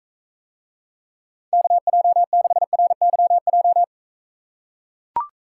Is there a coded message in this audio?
Yes: Morse code from 1.5 to 3.8 s, heard as a keyed tone, about 41 wpm, 700 Hz. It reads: KJ6RYJ